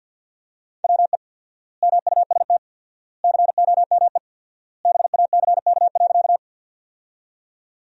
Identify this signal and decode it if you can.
Morse code, heard as a keyed tone, about 49 wpm, 700 Hz. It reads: WE MUST COME BACK.
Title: 4x Word-Spacing - Sentences from Top 200 Words - 50wpm